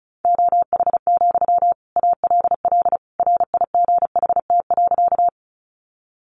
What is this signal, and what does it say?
Morse code, heard as a keyed tone, about 34 wpm, 700 Hz. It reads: OH, ALL RIGHT.
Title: Sentences from Top 500 Words 35wpm